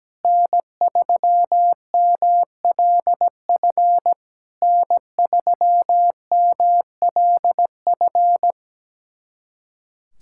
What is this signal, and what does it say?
Morse code, heard as a keyed tone, about 17 wpm, 700 Hz. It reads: N3MLF N3MLF
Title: SOTA QSO Practice 17wpm